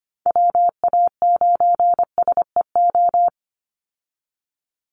Morse code, heard as a keyed tone, about 25 wpm, 700 Hz. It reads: WA9SEO
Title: Call Signs Repeated 3x - 25wpm